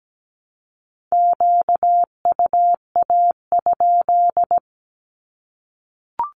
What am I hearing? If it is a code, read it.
Morse code, heard as a keyed tone, about 17 wpm, 700 Hz. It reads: QUA?
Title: Q-Codes Repeated 3x 17wpm